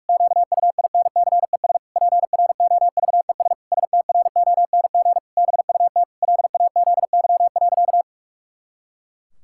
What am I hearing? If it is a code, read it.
Morse code, heard as a keyed tone, about 46 wpm, 700 Hz. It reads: QUINCES PROVES STRONG BUT LAZY.
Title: Pangram Challenge v2 - 45wpm